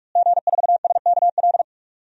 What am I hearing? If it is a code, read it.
Morse code, heard as a keyed tone, about 44 wpm, 700 Hz. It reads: G4SKL
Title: DXCC-100 Call Signs - 5 Characters - Repeated 3x - 45wpm